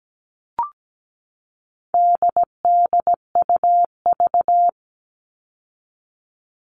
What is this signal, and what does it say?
Morse code, heard as a keyed tone, about 17 wpm, 700 Hz. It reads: DDUV
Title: All Cut Numbers - 4-Digit 17wpm